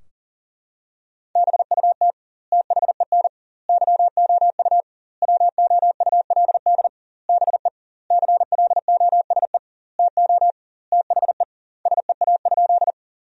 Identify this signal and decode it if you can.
Morse code, heard as a keyed tone, about 40 wpm, 700 Hz. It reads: BUT THEN YOU WOULD BE CLOSE TO THE SEA?
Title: 2x Word-Spacing - Sentences from Top 300 Words - 40wpm